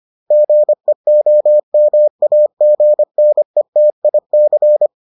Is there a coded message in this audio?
Yes: Morse code from 0.3 to 4.9 s, heard as a keyed tone, about 25 wpm, 600 Hz. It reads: GEOMAGNETIC